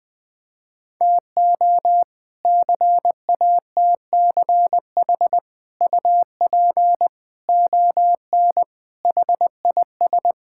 Morse code, heard as a keyed tone, about 21 wpm, 700 Hz. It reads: TO CATCH UP ON HIS
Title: All-Age Appropriate Jokes - 20wpm